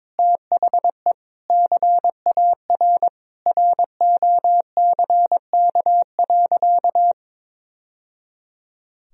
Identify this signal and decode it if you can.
Morse code, heard as a keyed tone, about 22 wpm, 700 Hz. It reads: THE CAR ROCK.
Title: Sentences from Top 1000 Words - Review Part 1 - 22wpm